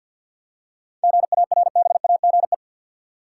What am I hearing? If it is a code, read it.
Morse code, heard as a keyed tone, about 52 wpm, 700 Hz. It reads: GARBAGE
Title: Seven Letter Words - 50wpm